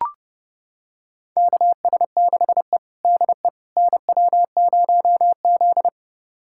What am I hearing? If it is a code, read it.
Morse code, heard as a keyed tone, about 30 wpm, 700 Hz. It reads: KS6E DE NW0Z
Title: USA Callsign DE Callsign - 4 Characters - 30wpm